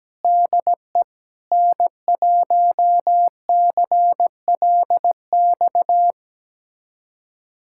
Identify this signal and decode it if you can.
Morse code, heard as a keyed tone, about 17 wpm, 700 Hz. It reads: DE N1CLX
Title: USA Callsign DE Callsign - 5 Characters - 17wpm